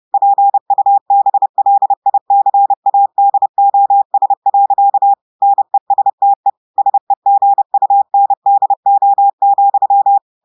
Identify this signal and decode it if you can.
Morse code, heard as a keyed tone, about 30 wpm, 800 Hz. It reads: PUBLICADOS. NESTE SEGUNDO,
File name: audios / portugais / portugais texte en clair 1 vitesse 30 mots minutes